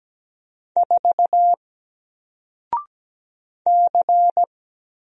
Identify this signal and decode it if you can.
Morse code, heard as a keyed tone, about 17 wpm, 700 Hz. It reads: 4 C
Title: Single Character (Letter:Number:Punctuation:Prosign) 17wpm